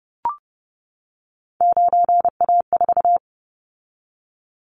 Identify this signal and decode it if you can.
Morse code, heard as a keyed tone, about 30 wpm, 700 Hz. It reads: 9A4
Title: Callsign Prefixes for Top 50 DXCC - 30wpm